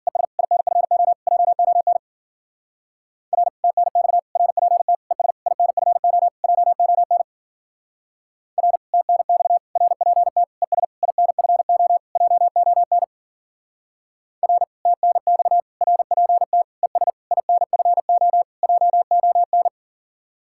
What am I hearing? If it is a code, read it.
Morse code, heard as a keyed tone, about 47 wpm, 700 Hz. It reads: ES INFO JON R TNX RPT ES INFO JON R TNX RPT ES INFO JON R TNX RPT ES INFO JON
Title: Speed-Racing Common QSO Phrases - 40wpm